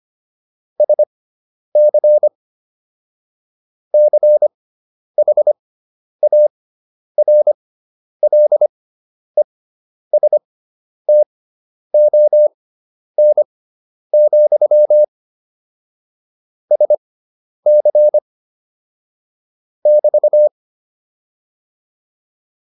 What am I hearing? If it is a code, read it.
Morse code, heard as a keyed tone, about 25 wpm, spaced as for 10 wpm, 600 Hz. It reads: SC CHARLESTON, SC <BT>
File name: QSO303 10